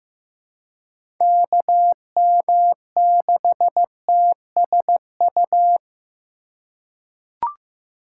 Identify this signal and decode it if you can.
Morse code, heard as a keyed tone, about 16 wpm, 700 Hz. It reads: KM6TSU
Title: USA Call Signs - 6 Characters - 3x Character Spacing - 15wpm